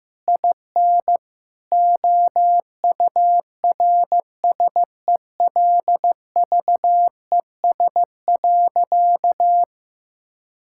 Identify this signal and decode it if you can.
Morse code, heard as a keyed tone, about 15 wpm, 700 Hz. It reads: IN OURSELVES.
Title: Sentences from Top 1600 Words 15wpm